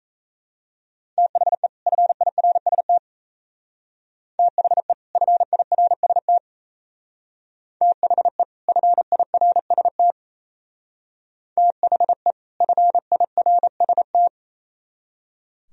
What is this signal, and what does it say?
Morse code, heard as a keyed tone, about 32 wpm, 700 Hz. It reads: THE FIRST THE FIRST THE FIRST THE FIRST
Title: Speed-Racing - Sets of 2 Words 28wpm